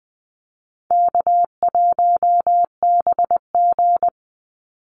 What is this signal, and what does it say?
Morse code, heard as a keyed tone, about 20 wpm, 700 Hz. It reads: K1BG
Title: Easy Call Signs Repeated 3x - 20wpm